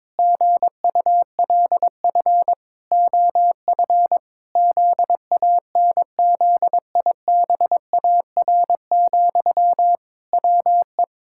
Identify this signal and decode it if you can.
Morse code, heard as a keyed tone, about 22 wpm, 700 Hz. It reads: GULF OF ZANZIBAR, WE
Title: Pangram Challenge v2 - 22wpm